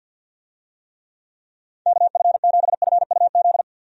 Morse code, heard as a keyed tone, about 48 wpm, 700 Hz. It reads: KF7FUZ
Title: USA Call Signs - 6 Characters - 50wpm